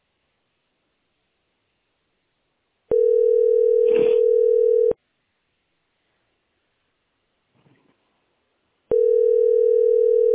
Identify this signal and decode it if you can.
Ringback tone (North America), 440 + 480 Hz.